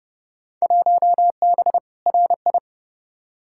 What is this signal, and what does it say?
Morse code, heard as a keyed tone, about 31 wpm, 700 Hz. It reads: 1B RI
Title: Field Day Exchanges Encore - 30wpm